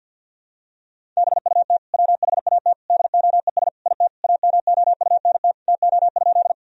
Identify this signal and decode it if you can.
Morse code, heard as a keyed tone, about 50 wpm, 700 Hz. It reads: BUT WHAT DOES IT AMOUNT TO?
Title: Sentences from Top 1000 Words - Review Part 4 - 50wpm